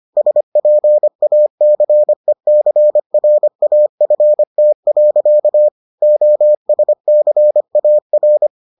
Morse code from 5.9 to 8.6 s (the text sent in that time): OSCAR